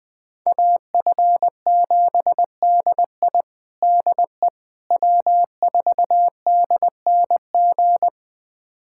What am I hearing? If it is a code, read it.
Morse code, heard as a keyed tone, about 20 wpm, 700 Hz. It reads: AF7DI DE W4DNG
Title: USA Callsign DE Callsign - 5 Characters - 20wpm